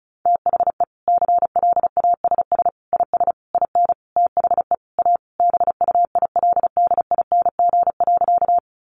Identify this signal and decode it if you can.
Morse code, heard as a keyed tone, about 34 wpm, 700 Hz. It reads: THE CLASS IS IN THE A BUILDING.
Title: Additional Easy Phrases 35wpm